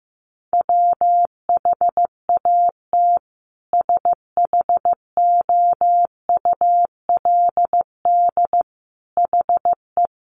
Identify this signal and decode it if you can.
Morse code, heard as a keyed tone, about 15 wpm, 700 Hz. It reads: WHAT SHOULD HE
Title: Sentences from Top 300 Words 15wpm